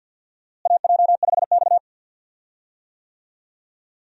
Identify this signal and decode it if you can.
Morse code, heard as a keyed tone, about 46 wpm, 700 Hz. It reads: AJ5X
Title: USA Call Signs - 4 Characters - Repeated 2x - 50wpm